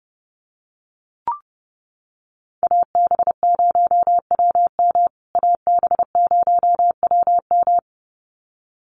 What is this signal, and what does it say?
Morse code, heard as a keyed tone, about 30 wpm, 700 Hz. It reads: AB0WM AB0WM
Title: USA Call Signs Repeated 2x 30wpm